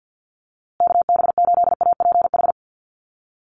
Morse code, heard as a keyed tone, about 50 wpm, 700 Hz. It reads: KB8APH
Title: USA Call Signs Repeated 3x 50wpm